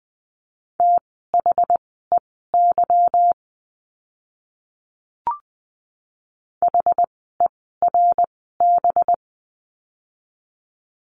Morse code, heard as a keyed tone, about 20 wpm, spaced as for 14 wpm, 700 Hz. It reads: THEY HERB
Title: 185 Lesson 30f - Letter b - Words 20wpm